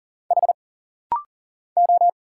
Morse code, heard as a keyed tone, about 40 wpm, 700 Hz. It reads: H O
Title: Single Letter-Number Alt - Rapid-Fire 40wpm